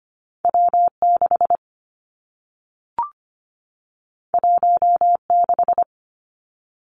Morse code, heard as a keyed tone, about 25 wpm, 700 Hz. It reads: W 6 1 6
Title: Two Challenging Letter-Number Combinations 25wpm